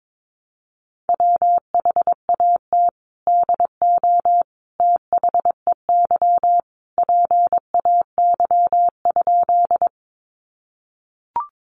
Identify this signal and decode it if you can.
Morse code, heard as a keyed tone, about 22 wpm, 700 Hz. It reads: WHAT DO THEY PAY?
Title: Sentences from Top 700 Words 22wpm